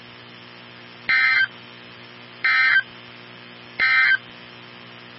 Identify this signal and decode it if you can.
Emergency Alert System (SAME) tones, heard as harsh buzzing bursts. It is heard from 1.0 to 4.2 s.
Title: NOAA Weather Radio 162.550